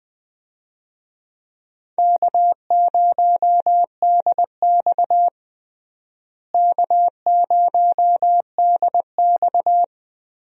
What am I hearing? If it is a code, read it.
Morse code, heard as a keyed tone, about 20 wpm, 700 Hz. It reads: K0DX K0DX
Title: USA Call Signs - 4 Characters - Repeated 2x - 20wpm